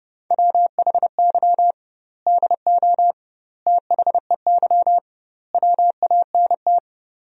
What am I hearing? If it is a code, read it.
Morse code, heard as a keyed tone, about 32 wpm, 700 Hz. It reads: WHY DO THEY WANT
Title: 2x Word-Spacing - Sentences from Top 400 Words - 30wpm